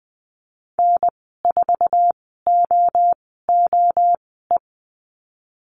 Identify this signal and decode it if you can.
Morse code, heard as a keyed tone, about 20 wpm, spaced as for 15 wpm, 700 Hz. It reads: N4OOE
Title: 062 Lesson 11g - Letter h - Call Signs 20wpm